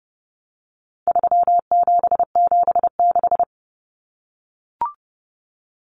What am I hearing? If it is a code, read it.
Morse code, heard as a keyed tone, about 30 wpm, 700 Hz. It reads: 3776
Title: Numbers - 4-Digit 30wpm